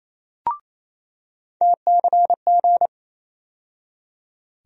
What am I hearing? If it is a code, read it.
Morse code, heard as a keyed tone, about 28 wpm, 700 Hz. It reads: TCG
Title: USA Call Sign Postfixes - 3 Letters 28wpm